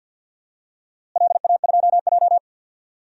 Morse code, heard as a keyed tone, about 47 wpm, 700 Hz. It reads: LA2J
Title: DXCC-100 Call Signs - 4 Characters - Repeated 2x - 50wpm